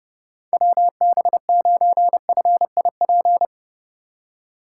Morse code, heard as a keyed tone, about 31 wpm, 700 Hz. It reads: WB9FIP
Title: K1USN SST QSO Practice - 30wpm